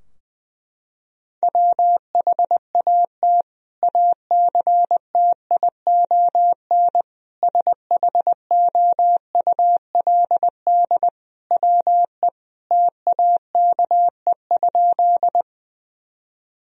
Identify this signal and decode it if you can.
Morse code, heard as a keyed tone, about 20 wpm, 700 Hz. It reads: WHAT ACTION SHOULD WE TAKE?
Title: Sentences from Top 1000 Words - Review Part 4 - 20wpm